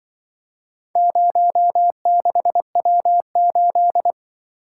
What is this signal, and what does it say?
Morse code, heard as a keyed tone, about 24 wpm, 700 Hz. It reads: 06W8
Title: Speed-Racing - Four Letter-Number Combinations - 20wpm